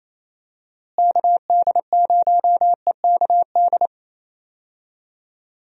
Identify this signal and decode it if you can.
Morse code, heard as a keyed tone, about 28 wpm, 700 Hz. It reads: KD0EKD
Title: USA Call Signs - 6 Characters - Repeated 3x - 28wpm